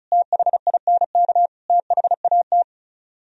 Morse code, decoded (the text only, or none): THINK THAT